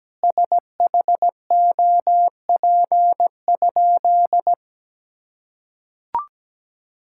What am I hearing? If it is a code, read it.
Morse code, heard as a keyed tone, about 17 wpm, 700 Hz. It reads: SHOP?
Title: Sentences from Top 1600 Words 17wpm